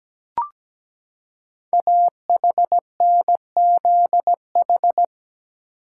Morse code, heard as a keyed tone, about 17 wpm, 700 Hz. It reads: AHNZH
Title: Code Groups - 5 Letters - 17wpm